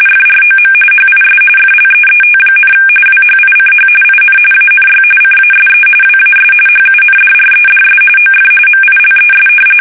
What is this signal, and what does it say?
RTTY, about 45.45 baud. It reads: FROM HIS FLEET, WHICH IN ANY CASE WAS HUNDREDS OF MILES
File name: fsk850 / battleshipnj